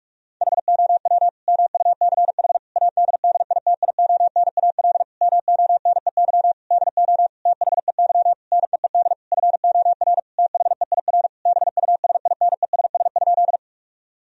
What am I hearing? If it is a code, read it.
Morse code, heard as a keyed tone, about 42 wpm, 700 Hz. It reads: HOW MUCH ADDITIONAL MONEY DO THEY NEED FOR THEIR BUSINESS?
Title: Sentences from Top 1000 Words - Review Part 4 - 45wpm